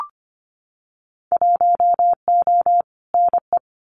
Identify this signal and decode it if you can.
Morse code, heard as a keyed tone, about 25 wpm, 700 Hz. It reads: 1O NE
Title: Winter Field Day Exchanges Encore 25wpm